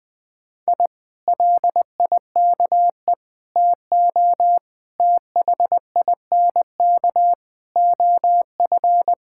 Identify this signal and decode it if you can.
Morse code, heard as a keyed tone, about 20 wpm, 700 Hz. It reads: I LIKE TO THINK OF
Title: Sentences with 8-Word Phrases - 20wpm